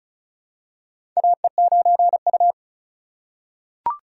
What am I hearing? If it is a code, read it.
Morse code, heard as a keyed tone, about 36 wpm, 700 Hz. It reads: AE9U
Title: USA Call Signs - 4 Characters - 4x Character Spacing - 35wpm